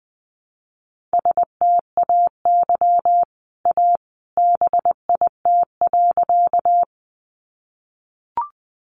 Morse code, heard as a keyed tone, about 20 wpm, 700 Hz. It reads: STAY A BIT.
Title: Sentences from Top 900 Words 20wpm